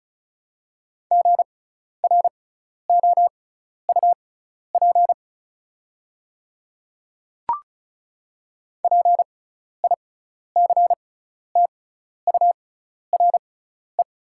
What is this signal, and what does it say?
Morse code, heard as a keyed tone, about 36 wpm, spaced as for 12 wpm, 700 Hz. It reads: GROUP PICTURE
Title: 6x Character Spacing - Top 500 Words - 35wpm